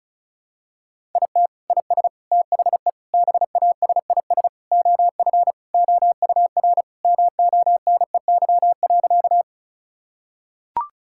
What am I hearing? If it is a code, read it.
Morse code, heard as a keyed tone, about 36 wpm, 700 Hz. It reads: IT IS THE BASIS OF OUR MONEY.
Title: Sentences from Top 1000 Words - Review Part 5 - 35wpm